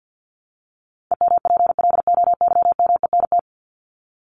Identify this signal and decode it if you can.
Morse code, heard as a keyed tone, about 50 wpm, 700 Hz. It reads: EMPLOYMENT
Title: Top 1251-1500 Words 50wpm